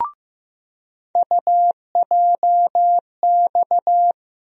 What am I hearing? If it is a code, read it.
Morse code, heard as a keyed tone, about 15 wpm, 700 Hz. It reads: UJX
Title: USA Call Sign Postfixes - 3 Letters 15wpm